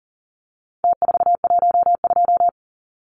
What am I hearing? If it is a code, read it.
Morse code, heard as a keyed tone, about 40 wpm, 700 Hz. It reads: T412
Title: Numbers - Contest Style - 4-Digit 40wpm